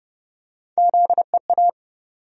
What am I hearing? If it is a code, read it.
Morse code, heard as a keyed tone, about 30 wpm, 700 Hz. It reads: ZEA
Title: USA Call Sign Postfixes - 3 Letters 30wpm